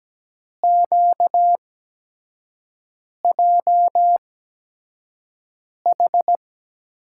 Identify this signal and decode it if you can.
Morse code, heard as a keyed tone, about 17 wpm, spaced as for 5 wpm, 700 Hz. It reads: QJH
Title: USA Call Signs - 5 Characters - 8x Character Spacing - 17wpm